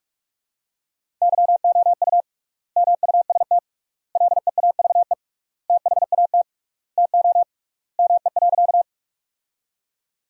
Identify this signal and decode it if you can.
Morse code, heard as a keyed tone, about 46 wpm, 700 Hz. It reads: YOU MUST LEAVE THAT TO ME.
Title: 3x Word-Spacing - Sentences from Top 400 Words - 45wpm